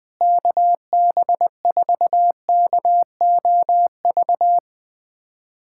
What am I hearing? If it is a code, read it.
Morse code, heard as a keyed tone, about 21 wpm, 700 Hz. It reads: KB4KOV
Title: USA Callsign DE Callsign - 6 Characters - 20wpm